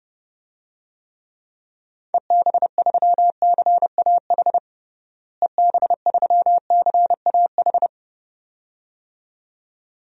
Morse code, heard as a keyed tone, about 30 wpm, 700 Hz. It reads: EB3CAH EB3CAH
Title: DXCC-100 Call Signs - 6 Characters - Repeated 2x - 30wpm